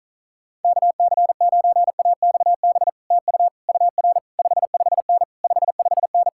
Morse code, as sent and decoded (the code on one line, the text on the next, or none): -.- -.-. ----. .- -..- -... / - ..- / ..- .-. / ..... ..... -. / ..... ..... -.
KC9AXB TU UR 55N 55N